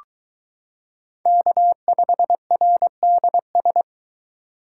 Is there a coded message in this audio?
Yes: Morse code from 1.3 to 3.8 s, heard as a keyed tone, about 23 wpm, 700 Hz. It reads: K5RDS